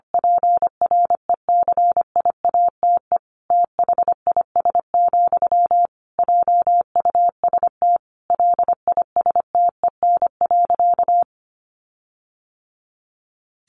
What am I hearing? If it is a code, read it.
Morse code, heard as a keyed tone, about 25 wpm, 700 Hz. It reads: PRECIATE THIS, JUST LISTEN.
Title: Sentences from Top 1300 Words 25wpm